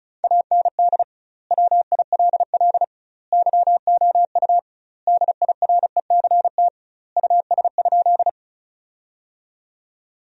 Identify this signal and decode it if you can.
Morse code, heard as a keyed tone, about 36 wpm, 700 Hz. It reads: AND WILL YOU DIRECT US?
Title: 2x Word-Spacing - Sentences from Top 400 Words - 35wpm